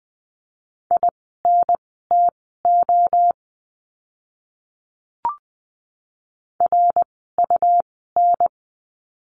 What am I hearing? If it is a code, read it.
Morse code, heard as a keyed tone, about 20 wpm, spaced as for 14 wpm, 700 Hz. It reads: INTO RUN
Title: 237 Lesson 38f - Letter q - Words 20wpm